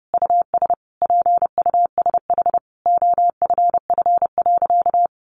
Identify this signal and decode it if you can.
Morse code, heard as a keyed tone, about 30 wpm, 700 Hz. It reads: US PUSH OFF.
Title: Sentences from Top 1300 Words 30wpm